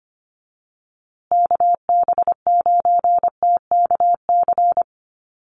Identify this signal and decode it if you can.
Morse code, heard as a keyed tone, about 25 wpm, 700 Hz. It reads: KB9TKC
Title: USA Call Signs Repeated 2x 25wpm